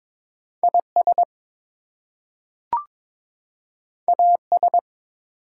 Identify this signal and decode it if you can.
Morse code, heard as a keyed tone, about 22 wpm, 700 Hz. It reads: IS AS
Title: Two Letter Words - 22wpm